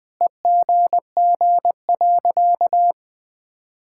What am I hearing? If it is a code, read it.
Morse code, heard as a keyed tone, about 19 wpm, 700 Hz. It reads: EGG.
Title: Sentences with 7-Word Phrases - 20wpm